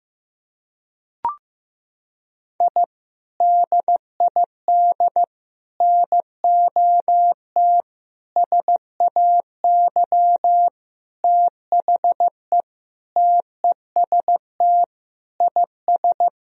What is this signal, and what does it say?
Morse code, heard as a keyed tone, about 16 wpm, 700 Hz. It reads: I DID NOT SAY THE TEST IS
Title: Sentences from Top 1000 Words - Review Part 3 - 15wpm